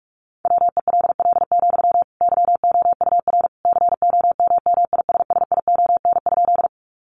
Morse code, heard as a keyed tone, about 44 wpm, 700 Hz. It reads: WELL, YOUR COMMISSION?
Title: Sentences from Top 1400 Words 45wpm